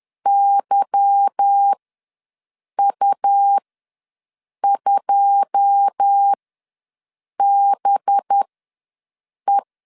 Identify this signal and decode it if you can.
Morse code, heard as a keyed tone, about 10 wpm, spaced as for 6 wpm, 800 Hz. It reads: YU2BE
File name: LEZIONI / 9 2 ws 10wpm 4130